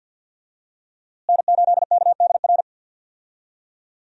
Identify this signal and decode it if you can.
Morse code, heard as a keyed tone, about 52 wpm, 700 Hz. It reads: N8KDR
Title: USA Call Signs - 5 Characters - Repeated 3x - 50wpm